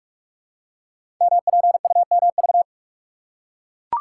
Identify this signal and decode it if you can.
Morse code, heard as a keyed tone, about 44 wpm, 690 Hz. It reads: MPUMV
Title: Code Groups - 5 Letters - 45wpm